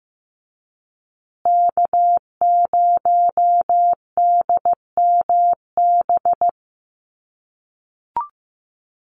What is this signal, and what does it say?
Morse code, heard as a keyed tone, about 15 wpm, 700 Hz. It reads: K0DMB
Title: USA Call Signs Repeated 2x 15wpm